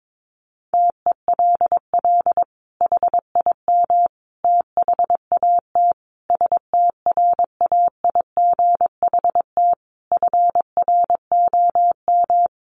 Morse code, heard as a keyed tone, about 22 wpm, 700 Hz. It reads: TELL HIM THAT STRAIGHT FROM
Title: Sentences from Top 1500 Words 22wpm